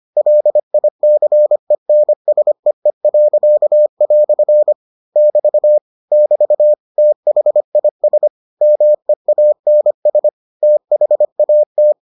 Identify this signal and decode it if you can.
Morse code, heard as a keyed tone, about 25 wpm, 600 Hz. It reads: LICENSEE." <BT> <BT> THIS MEANS THAT